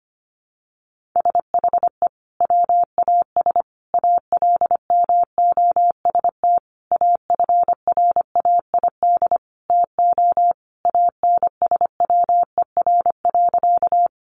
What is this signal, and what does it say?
Morse code, heard as a keyed tone, about 25 wpm, 700 Hz. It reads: SHE WAS ALMOST AFRAID TO ANSWER.
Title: Sentences from Top 1500 Words 25wpm